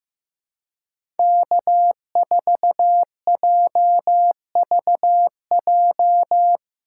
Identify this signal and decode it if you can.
Morse code, heard as a keyed tone, about 15 wpm, 700 Hz. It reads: K4JVJ
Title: USA Call Signs - 5 Characters - 4x Character Spacing - 15wpm